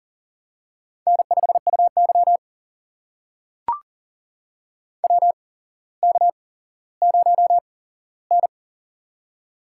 Morse code, heard as a keyed tone, about 39 wpm, spaced as for 17 wpm, 700 Hz. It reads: N5UY WK0N
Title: USA Call Signs - 4 Characters - 8x Character Spacing - 40wpm